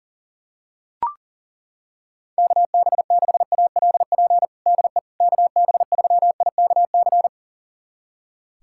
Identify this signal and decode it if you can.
Morse code, heard as a keyed tone, about 40 wpm, 700 Hz. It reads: KB6ALP DE KB3IKC
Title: USA Callsign DE Callsign 40wpm